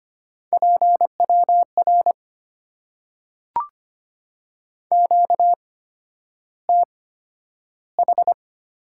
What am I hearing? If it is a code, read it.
Morse code, heard as a keyed tone, about 25 wpm, spaced as for 11 wpm, 700 Hz. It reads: PWR QTH
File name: Top 90 QSO Elements - 8x Character Spacing - 25wpm